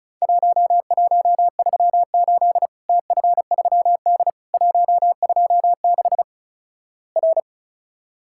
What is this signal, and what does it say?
Morse code, heard as a keyed tone, about 36 wpm, 700 Hz. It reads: 1138 TF3D 126 R
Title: Worked All Europe DX Contest - QTC Exchange - 35wpm